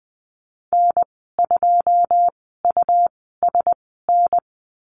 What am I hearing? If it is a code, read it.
Morse code, heard as a keyed tone, about 19 wpm, spaced as for 14 wpm, 700 Hz. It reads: N2USN
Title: 167 Lesson 27g - Number 7 - Call Signs 20wpm